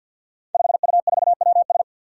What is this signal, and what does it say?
Morse code, heard as a keyed tone, about 46 wpm, 700 Hz. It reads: 5U4WS
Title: Five Challenging Letter-Number Combinations 50wpm